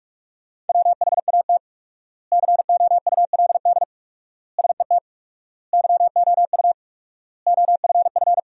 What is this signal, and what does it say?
Morse code, heard as a keyed tone, about 47 wpm, 700 Hz. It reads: WHAT COULD SET YOU OFF
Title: 4x Word-Spacing - Sentences from Top 200 Words - 45wpm